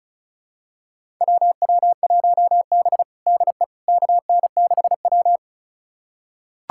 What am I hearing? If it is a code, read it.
Morse code, heard as a keyed tone, about 36 wpm, 700 Hz. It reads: WW1B DE KN6W
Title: USA Callsign DE Callsign - 4 Characters - 35wpm